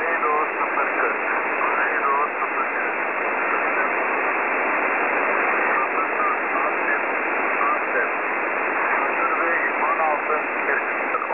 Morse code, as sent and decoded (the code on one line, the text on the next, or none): none